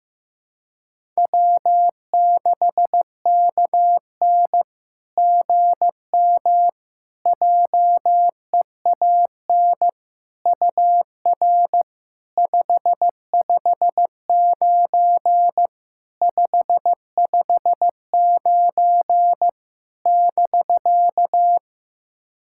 Morse code, heard as a keyed tone, about 15 wpm, 690 Hz. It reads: W6KN GM JEAN UR 559 559 <BK>
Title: SOTA QSO Practice 15wpm